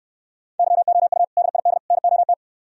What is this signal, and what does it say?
Morse code, heard as a keyed tone, about 68 wpm, 700 Hz. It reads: YOU DID NOT